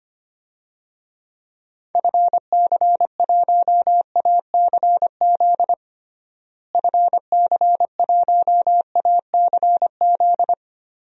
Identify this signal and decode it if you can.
Morse code, heard as a keyed tone, about 25 wpm, 700 Hz. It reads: FC1ACZ FC1ACZ
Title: DXCC-100 Call Signs - 6 Characters - Repeated 2x - 25wpm